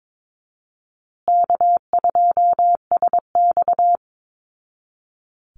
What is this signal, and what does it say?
Morse code, heard as a keyed tone, about 22 wpm, 700 Hz. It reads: K2SX
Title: Call Signs Repeated 3x - 22wpm